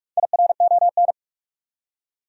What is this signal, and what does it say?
Morse code, heard as a keyed tone, about 46 wpm, 700 Hz. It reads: IRON
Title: Four Letter Words - 45wpm